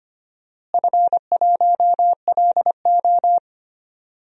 Morse code, heard as a keyed tone, about 25 wpm, 700 Hz. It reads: F1LO
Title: DXCC-100 Call Signs - 4 Characters - Repeated 2x - 25wpm